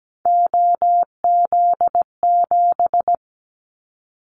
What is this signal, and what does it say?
Morse code, heard as a keyed tone, about 17 wpm, 700 Hz. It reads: OZ7
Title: Callsign Prefixes for Top 100 DXCC - 17wpm